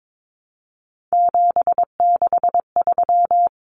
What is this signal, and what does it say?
Morse code, heard as a keyed tone, about 22 wpm, 700 Hz. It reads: 763
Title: Numbers - 3-Digit 22wpm